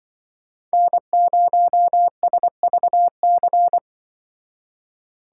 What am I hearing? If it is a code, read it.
Morse code, heard as a keyed tone, about 23 wpm, 700 Hz. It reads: N0SVC
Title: Speed Racing - USA 1x3 Call Signs 20wpm